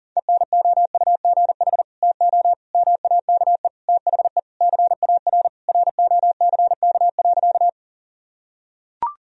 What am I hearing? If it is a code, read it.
Morse code, heard as a keyed tone, about 37 wpm, 700 Hz. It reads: ENOUGH TO MAKE THE CAR ROCK.
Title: Sentences from Top 1000 Words - Review Part 1 - 40wpm